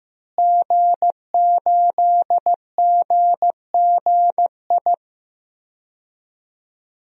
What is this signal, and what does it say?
Morse code, heard as a keyed tone, about 15 wpm, 700 Hz. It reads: G8GGI
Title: DXCC-100 Call Signs - 5 Characters - Repeated 2x - 15wpm